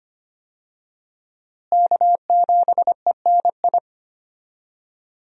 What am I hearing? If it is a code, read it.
Morse code, heard as a keyed tone, about 25 wpm, 700 Hz. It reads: K7ENI
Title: USA Call Signs - 5 Characters - Repeated 2x - 25wpm